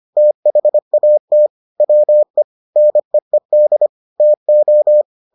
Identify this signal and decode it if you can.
Morse code, heard as a keyed tone, about 25 wpm, 600 Hz. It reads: THAT WE NEED TO